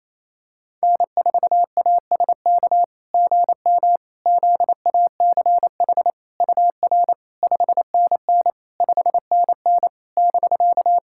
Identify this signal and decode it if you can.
Morse code, heard as a keyed tone, about 28 wpm, 690 Hz. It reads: N4ASK GM ZACH UR 5NN 5NN <BK>
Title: SOTA QSO Practice 28wpm